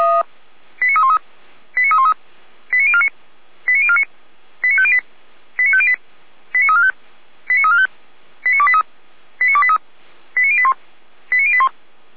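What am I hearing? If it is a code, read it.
Five-tone selective call (ZVEI), heard as quick runs of five tones, tones of about 70 ms. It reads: 89313 89313 89049 89049 89508 89508 89345 89345 89283 89283 89081 89081
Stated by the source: ZVEI five-tone call 89305 received on SDR, Feuerwehr Orlamünde.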